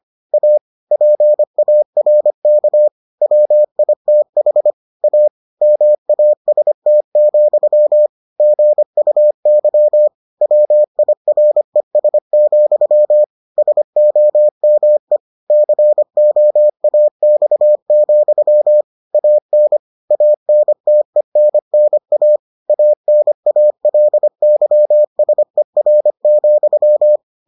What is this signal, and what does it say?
Morse code, heard as a keyed tone, about 25 wpm, 600 Hz. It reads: A PARK WITH A MAST, GUY WIRES, SOME COAX, AN ANTENNA ANALYSER,